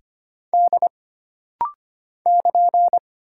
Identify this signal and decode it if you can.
Morse code, heard as a keyed tone, about 25 wpm, 700 Hz. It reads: D <KN>
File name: Single Character (Letter:Number:Punctuation:Prosign) - Rapid-Fire - 25wpm